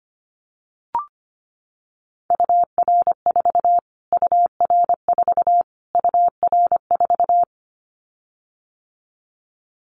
Morse code, heard as a keyed tone, about 25 wpm, 700 Hz. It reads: UR4 UR4 UR4
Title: Callsign Prefixes for Top 25 DXCC - Repeated 3x - 25wpm